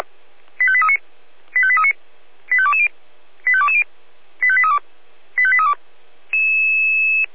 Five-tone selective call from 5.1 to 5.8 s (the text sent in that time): 85832